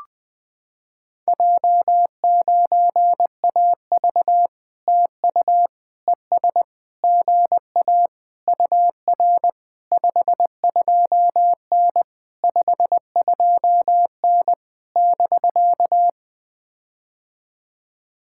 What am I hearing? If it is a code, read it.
Morse code, heard as a keyed tone, about 21 wpm, 700 Hz. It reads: J9AV TU ES GA UR 52N 52N <BK>
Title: POTA QSO Practice - 20wpm